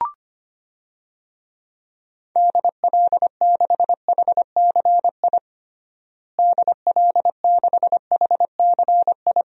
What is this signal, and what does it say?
Morse code, heard as a keyed tone, about 25 wpm, 700 Hz. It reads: DL6HCI DL6HCI
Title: DXCC-100 Call Signs - 6 Characters - Repeated 3x - 25wpm